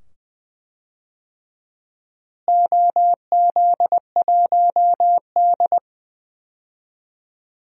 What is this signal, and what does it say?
Morse code, heard as a keyed tone, about 20 wpm, 700 Hz. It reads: OZ1D
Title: DXCC-100 Call Signs - 4 Characters - Repeated 2x - 20wpm